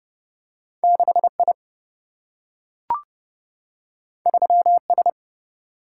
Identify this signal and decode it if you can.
Morse code, heard as a keyed tone, about 30 wpm, 700 Hz. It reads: 6 I 3 S
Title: Single-Letter Single-Number 30wpm